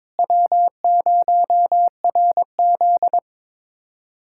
Morse code, heard as a keyed tone, about 22 wpm, 700 Hz. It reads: W0RZ
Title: USA Callsign DE Callsign - 4 Characters - 22wpm